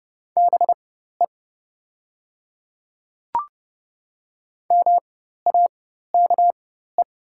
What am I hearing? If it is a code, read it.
Morse code, heard as a keyed tone, about 30 wpm, spaced as for 14 wpm, 700 Hz. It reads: BE MAKE
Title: 4x Character Spacing - Top 100 Words - 30wpm